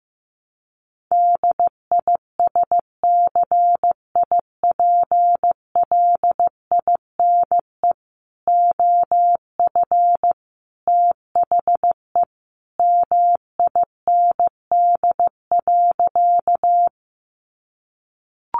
Morse code, heard as a keyed tone, about 15 wpm, 700 Hz. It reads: DISCIPLINE OF THE MIND.
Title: Sentences from Top 900 Words 15wpm